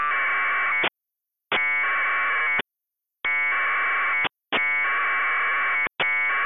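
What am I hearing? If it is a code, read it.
Packet radio (AX.25, as in APRS).
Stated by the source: APRS packet frames received and recorded on air by a digipeater.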